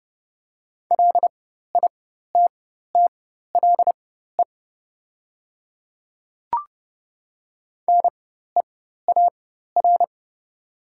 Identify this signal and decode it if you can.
Morse code, heard as a keyed tone, about 30 wpm, spaced as for 14 wpm, 700 Hz. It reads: LITTLE NEAR
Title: 4x Character Spacing - Top 200 Words - 30wpm